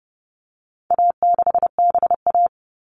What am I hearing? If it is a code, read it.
Morse code, heard as a keyed tone, about 28 wpm, 700 Hz. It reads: A6BA
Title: All Cut Numbers - 4-Digit 30wpm